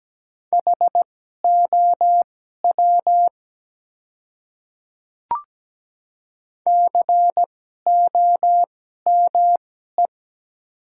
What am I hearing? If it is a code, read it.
Morse code, heard as a keyed tone, about 17 wpm, spaced as for 12 wpm, 700 Hz. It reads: HOW COME
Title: 2x Character Spacing - Top 100 Words - 17wpm